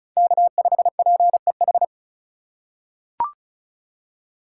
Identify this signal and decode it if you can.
Morse code, heard as a keyed tone, about 36 wpm, 700 Hz. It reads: K5PEH
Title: USA Call Signs - 4x Character Spacing - 35wpm